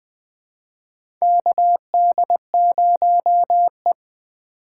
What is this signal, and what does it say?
Morse code, heard as a keyed tone, about 20 wpm, 700 Hz. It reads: KD0E
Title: USA Call Signs - 4 Characters - 4x Character Spacing - 20wpm